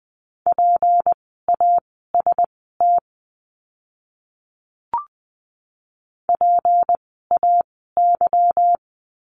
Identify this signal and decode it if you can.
Morse code, heard as a keyed tone, about 20 wpm, spaced as for 15 wpm, 700 Hz. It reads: PAST PAY
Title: 153 Lesson 25f - Letter p - Words 20wpm